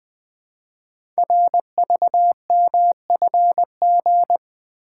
Morse code, heard as a keyed tone, about 20 wpm, 700 Hz. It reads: RVMFG
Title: Code Groups - 5 Letter:Numbers - 20wpm